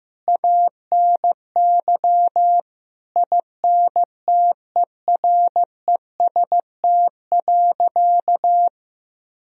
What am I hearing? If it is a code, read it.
Morse code, heard as a keyed tone, about 15 wpm, 700 Hz. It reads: ANY INTEREST.
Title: Sentences from Top 1000 Words - Review Part 2 - 15wpm